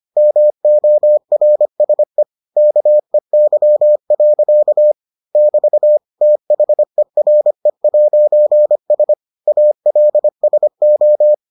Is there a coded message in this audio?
Yes: Morse code from 0.2 to 11.3 s, heard as a keyed tone, about 25 wpm, 600 Hz. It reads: MORSE KEY. <BT> THERE'S ALSO